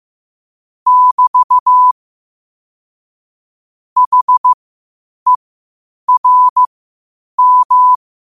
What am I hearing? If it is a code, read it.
Morse code, heard as a keyed tone, about 15 wpm, spaced as for 8 wpm, 1000 Hz. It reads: <BT> HERM
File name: qotdfiles / 8wpm